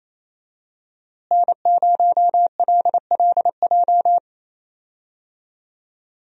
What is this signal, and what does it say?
Morse code, heard as a keyed tone, about 28 wpm, 700 Hz. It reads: N0LLJ
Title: USA Call Signs - 5 Characters - Repeated 2x - 28wpm